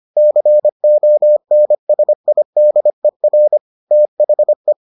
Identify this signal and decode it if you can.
Morse code, heard as a keyed tone, about 25 wpm, 600 Hz. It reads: CONSIDER THE